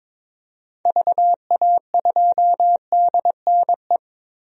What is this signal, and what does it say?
Morse code, heard as a keyed tone, about 23 wpm, 700 Hz. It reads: VA2DNE
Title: DXCC-100 Call Signs - 6 Characters - Repeated 2x - 22wpm